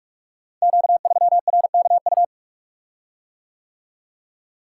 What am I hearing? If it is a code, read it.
Morse code, heard as a keyed tone, about 44 wpm, 700 Hz. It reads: Q3RKU
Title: Code Groups - 5 Letter:Numbers - 45wpm